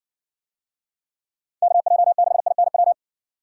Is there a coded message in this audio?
Yes: Morse code from 1.6 to 2.9 s, heard as a keyed tone, about 58 wpm, 700 Hz. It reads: KJ6INW